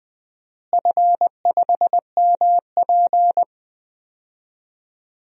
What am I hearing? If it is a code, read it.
Morse code, heard as a keyed tone, about 20 wpm, 700 Hz. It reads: F5MP
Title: DXCC-100 Call Signs - 4 Characters - Repeated 2x - 20wpm